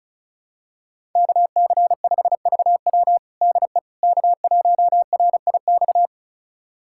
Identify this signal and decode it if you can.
Morse code, heard as a keyed tone, about 36 wpm, 700 Hz. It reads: KC5VW DE K1RIX
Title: USA Callsign DE Callsign 35wpm